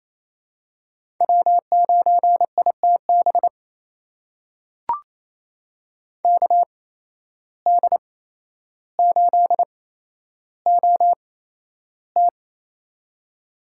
Morse code, heard as a keyed tone, about 29 wpm, spaced as for 12 wpm, 700 Hz. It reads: W9ITB KD8OT
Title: USA Call Signs - 5 Characters - 8x Character Spacing - 28wpm